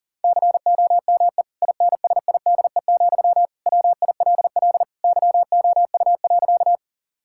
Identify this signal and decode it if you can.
Morse code, heard as a keyed tone, about 40 wpm, 700 Hz. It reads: COME INSIDE, WILL YOU.
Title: Sentences from Top 1000 Words - Review Part 6 - 40wpm